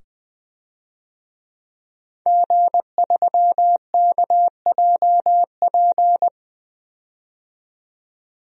Morse code, heard as a keyed tone, about 20 wpm, 700 Hz. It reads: G3KJP
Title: DXCC-100 Call Signs - 5 Characters - 20wpm